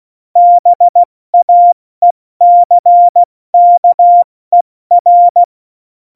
Morse code, heard as a keyed tone, about 16 wpm, 700 Hz. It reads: BAECKER
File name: Morsekurs / Witze / Witz 11. September 2018, 70 F80